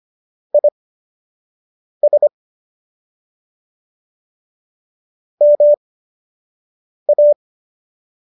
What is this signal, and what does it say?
Morse code, heard as a keyed tone, about 25 wpm, spaced as for 6 wpm, 590 Hz. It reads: IS MA